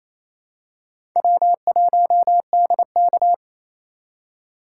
Morse code, heard as a keyed tone, about 28 wpm, 700 Hz. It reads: W1DK
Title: USA Call Signs - 8x Character Spacing - 28wpm